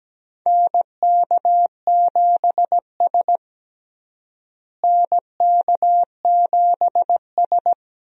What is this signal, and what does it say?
Morse code, heard as a keyed tone, about 17 wpm, 700 Hz. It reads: NK7S NK7S
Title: USA Call Signs - 4 Characters - Repeated 3x - 17wpm